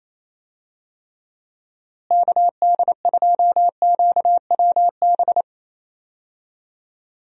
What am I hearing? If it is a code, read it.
Morse code, heard as a keyed tone, about 28 wpm, 700 Hz. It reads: KD2QWB
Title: USA Call Signs - 6 Characters - Repeated 3x - 28wpm